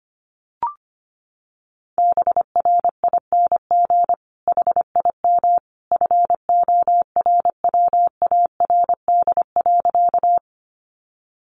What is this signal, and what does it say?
Morse code, heard as a keyed tone, about 25 wpm, 700 Hz. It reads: BRING HIM FORWARD.
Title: Sentences from Top 1300 Words 25wpm